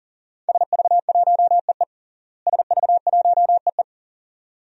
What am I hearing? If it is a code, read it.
Morse code, heard as a keyed tone, about 37 wpm, 700 Hz. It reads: SV1EE SV1EE
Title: DXCC-100 Call Signs - 5 Characters - Repeated 2x - 40wpm